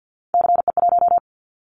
Morse code, heard as a keyed tone, about 48 wpm, 700 Hz. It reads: CE1
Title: Callsign Prefixes for Top 100 DXCC - 50wpm